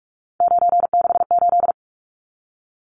Morse code, heard as a keyed tone, about 44 wpm, 700 Hz. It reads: 968
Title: Numbers - 3-Digit 45wpm